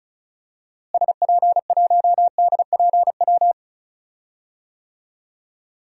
Morse code, heard as a keyed tone, about 36 wpm, 700 Hz. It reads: SP1DPW
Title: DXCC-100 Call Signs - 6 Characters - Repeated 2x - 35wpm